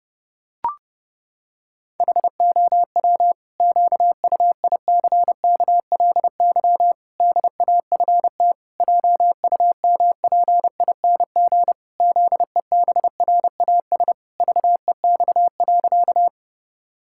Morse code, heard as a keyed tone, about 30 wpm, 700 Hz. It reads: HOW QUICKLY DAFT JUMPING ZEBRAS VEX.
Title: Pangram Challenge v2 - 30wpm